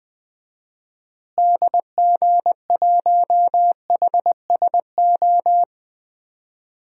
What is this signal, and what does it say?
Morse code, heard as a keyed tone, about 20 wpm, 700 Hz. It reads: DG1HSO
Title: DXCC-100 Call Signs - 6 Characters - Repeated 2x - 20wpm